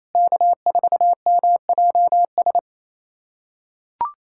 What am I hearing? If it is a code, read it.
Morse code, heard as a keyed tone, about 28 wpm, 700 Hz. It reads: K4MJS
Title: USA Callsign DE Callsign - 5 Characters - 28wpm